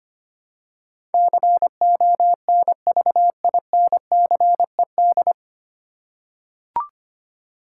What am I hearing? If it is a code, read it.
Morse code, heard as a keyed tone, about 25 wpm, 700 Hz. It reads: CONVINCED
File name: practice / 25wpm / 9 Letter Words - First 3 Letters Given - 25wpm